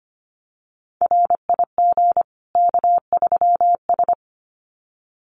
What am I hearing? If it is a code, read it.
Morse code, heard as a keyed tone, about 25 wpm, 700 Hz. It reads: RIG K3S
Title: Common QSO Phrases 25wpm